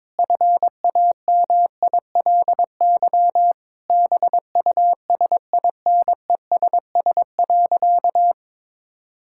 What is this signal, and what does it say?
Morse code, heard as a keyed tone, about 21 wpm, 700 Hz. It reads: FAMILY BUSINESS.
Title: Sentences from Top 1000 Words - Review Part 5 - 22wpm